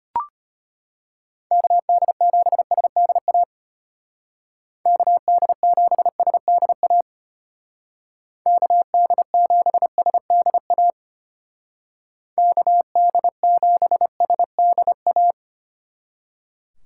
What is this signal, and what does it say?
Morse code, heard as a keyed tone, about 32 wpm, 700 Hz. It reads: KD7SDA KD7SDA KD7SDA KD7SDA